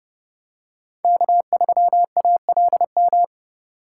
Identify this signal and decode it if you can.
Morse code, heard as a keyed tone, about 29 wpm, 700 Hz. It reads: K3ALM